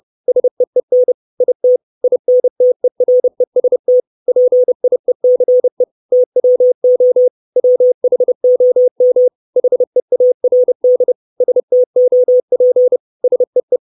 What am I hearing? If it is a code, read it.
Morse code, heard as a keyed tone, about 30 wpm, 500 Hz. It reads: SEEN IT INTEREST PIECE TWO WHOM HEARD STOP SEE
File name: 632 Words 30WPM